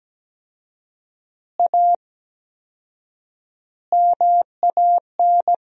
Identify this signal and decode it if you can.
Morse code, heard as a keyed tone, about 17 wpm, 700 Hz. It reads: A MAN
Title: 4x Word-Spacing - Sentences from Top 200 Words - 17wpm